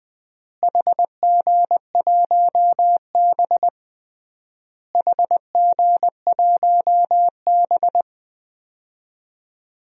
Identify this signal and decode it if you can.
Morse code, heard as a keyed tone, about 21 wpm, 700 Hz. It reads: HG1B HG1B
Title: DXCC-100 Call Signs - 4 Characters - Repeated 3x - 20wpm